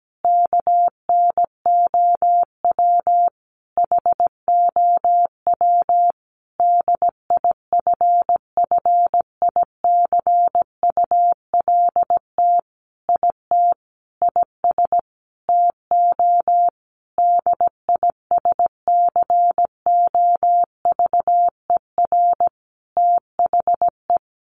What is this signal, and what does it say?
Morse code, heard as a keyed tone, about 17 wpm, 700 Hz. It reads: KNOW HOW DIFFICULT IT IS TO DISCOVER THE
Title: Sentences from Top 1100 Words 17wpm